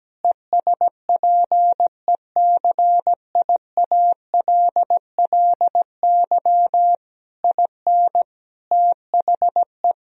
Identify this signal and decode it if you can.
Morse code, heard as a keyed tone, about 17 wpm, 700 Hz. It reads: ESPECIALLY IN THE
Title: Sentences from Top 1000 Words - Review Part 4 - 17wpm